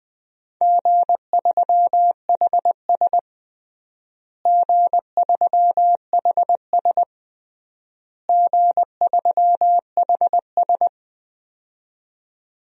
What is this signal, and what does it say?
Morse code, heard as a keyed tone, about 20 wpm, 700 Hz. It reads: G3HS G3HS G3HS
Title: DXCC-100 Call Signs - 4 Characters - Repeated 3x - 20wpm